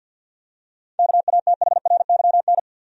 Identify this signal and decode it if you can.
Morse code, heard as a keyed tone, about 52 wpm, 700 Hz. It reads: KATHRYN
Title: USA Common Names - 50wpm